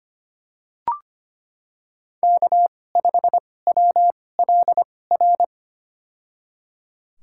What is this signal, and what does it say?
Morse code, heard as a keyed tone, about 25 wpm, spaced as for 18 wpm, 700 Hz. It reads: K5WLR
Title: USA Call Signs - 5 Characters - 2x Character Spacing - 25wpm